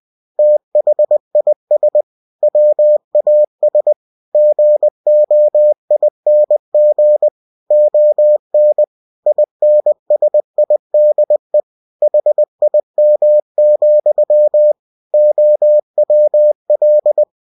Morse code, heard as a keyed tone, about 20 wpm, 600 Hz. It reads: THIS WAS GOING ON INSIDE HIM, OWL